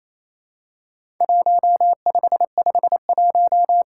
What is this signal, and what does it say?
Morse code, heard as a keyed tone, about 28 wpm, 700 Hz. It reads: 1551
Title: Worked All Europe DX Contest - QTC Exchange - 28wpm